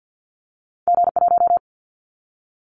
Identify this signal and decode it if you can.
Morse code, heard as a keyed tone, about 50 wpm, 700 Hz. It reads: G1
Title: Callsign Prefixes for Top 25 DXCC - 50wpm